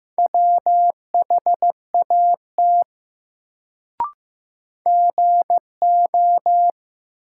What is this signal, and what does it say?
Morse code, heard as a keyed tone, about 15 wpm, 700 Hz. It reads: WHAT GO
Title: Sentences from Top 100 Words - New Word Drill - 15wpm